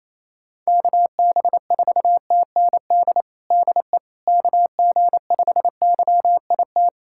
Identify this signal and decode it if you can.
Morse code, heard as a keyed tone, about 28 wpm, 700 Hz. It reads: KB4TND DE KG5YIT
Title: USA Callsign DE Callsign - 6 Characters - 28wpm